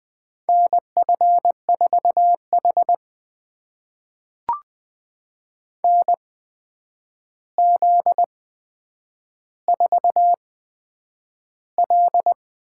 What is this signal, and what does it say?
Morse code, heard as a keyed tone, about 20 wpm, spaced as for 9 wpm, 700 Hz. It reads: NF4H NZ4L
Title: USA Call Signs - 4 Characters - 8x Character Spacing - 20wpm